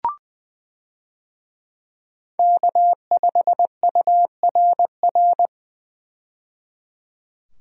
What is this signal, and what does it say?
Morse code, heard as a keyed tone, about 20 wpm, 700 Hz. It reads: K5URR
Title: USA Call Signs - 5 Characters - 20wpm